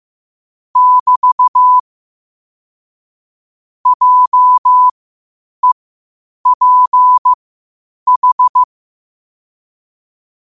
Morse code, heard as a keyed tone, about 15 wpm, spaced as for 8 wpm, 1000 Hz. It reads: <BT> JEPH
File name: qotdfiles / 8wpm